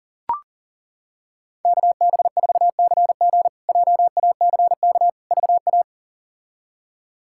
Morse code, heard as a keyed tone, about 40 wpm, 700 Hz. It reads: KB4CG JACK VA
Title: K1USN SST QSO Practice - 40wpm